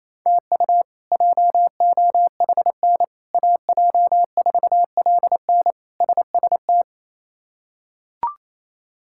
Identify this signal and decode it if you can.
Morse code, heard as a keyed tone, about 28 wpm, 700 Hz. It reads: TU JOHN AJ4LN SST
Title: K1USN SST QSO Practice - 28wpm